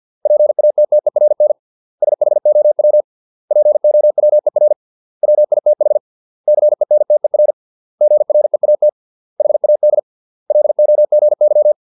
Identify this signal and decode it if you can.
Morse code, heard as a keyed tone, about 50 wpm, 600 Hz. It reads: PATTERN SHOW POWER WITH CENTER GREAT HAD LOGY